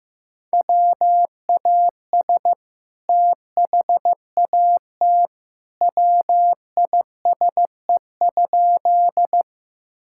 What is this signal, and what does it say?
Morse code, heard as a keyed tone, about 15 wpm, 700 Hz. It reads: WAS THAT WISE?
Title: Sentences from Top 1600 Words 15wpm